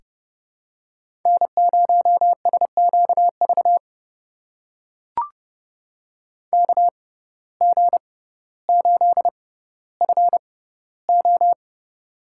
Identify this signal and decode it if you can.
Morse code, heard as a keyed tone, about 30 wpm, spaced as for 15 wpm, 700 Hz. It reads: N0SQV KG8FO
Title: USA Call Signs - 5 Characters - 6x Character Spacing - 30wpm